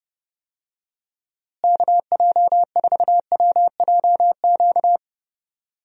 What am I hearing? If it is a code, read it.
Morse code, heard as a keyed tone, about 30 wpm, 700 Hz. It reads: KJ4WJQ
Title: USA Call Signs - 6 Characters - Repeated 2x - 30wpm